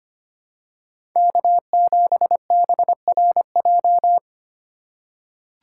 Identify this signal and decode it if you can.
Morse code, heard as a keyed tone, about 25 wpm, 700 Hz. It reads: K7BRJ
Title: USA Call Signs - 2x Character Spacing - 25wpm